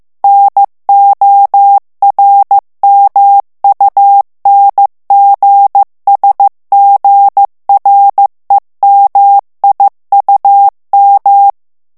Morse code, heard as a keyed tone, about 15 wpm, 800 Hz. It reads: NORMUNGSGREMIUM
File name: morse mp3 / ALLEMAND texte clair 1 vitesse 14 mots minute